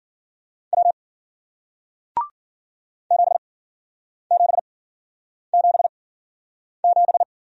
Morse code, heard as a keyed tone, about 53 wpm, spaced as for 9 wpm, 700 Hz. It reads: W 7777